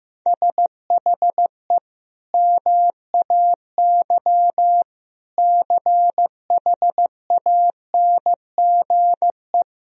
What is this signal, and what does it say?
Morse code, heard as a keyed tone, about 16 wpm, 700 Hz. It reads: SHE MAY CHANGE
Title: Sentences from Top 1000 Words - Review Part 1 - 15wpm